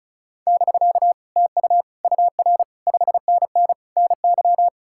Morse code, heard as a keyed tone, about 45 wpm, 700 Hz. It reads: <BK> TU UR 5NN NY